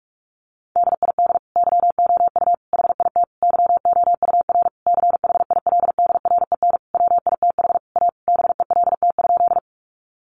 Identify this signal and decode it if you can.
Morse code, heard as a keyed tone, about 46 wpm, 700 Hz. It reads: DID YOU HIT YOUR CHILDREN WITH A BELT?
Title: Sentences from Top 1400 Words 45wpm